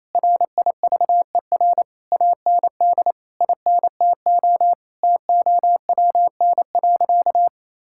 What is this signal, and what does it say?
Morse code, heard as a keyed tone, about 28 wpm, 700 Hz. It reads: RIVER AND INTO TOWN.
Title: Sentences from Top 1000 Words - Review Part 6 - 28wpm